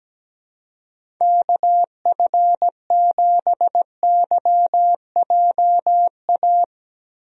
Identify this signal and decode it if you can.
Morse code, heard as a keyed tone, about 17 wpm, 700 Hz. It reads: KF7YJA
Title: USA Call Signs - 6 Characters - Repeated 3x - 17wpm